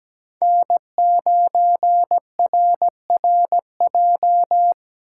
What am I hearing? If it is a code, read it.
Morse code, heard as a keyed tone, about 17 wpm, 700 Hz. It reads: N9RRJ
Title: USA Call Signs - 3x Character Spacing - 17wpm